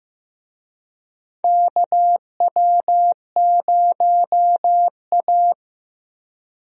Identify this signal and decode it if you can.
Morse code, heard as a keyed tone, about 15 wpm, 700 Hz. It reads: KW0A
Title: USA Call Signs - 4 Characters - Repeated 3x - 15wpm